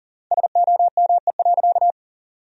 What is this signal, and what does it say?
Morse code, heard as a keyed tone, about 40 wpm, 700 Hz. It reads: SOME.
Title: 3x Word-Spacing - Sentences from Top 100 Words - 40wpm